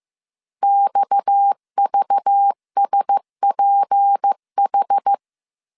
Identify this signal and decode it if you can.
Morse code, heard as a keyed tone, about 15 wpm, 800 Hz. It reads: XVSPH
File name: CW / DET / 1 6 1500C S B 14wpm 2958